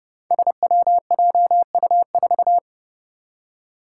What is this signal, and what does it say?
Morse code, heard as a keyed tone, about 29 wpm, 700 Hz. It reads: SWJU4
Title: Five Challenging Letter-Number Combinations 30wpm